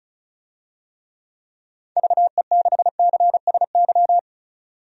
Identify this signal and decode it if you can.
Morse code, heard as a keyed tone, about 34 wpm, 700 Hz. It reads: VE6CSY
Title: DXCC-100 Call Signs - 6 Characters - 35wpm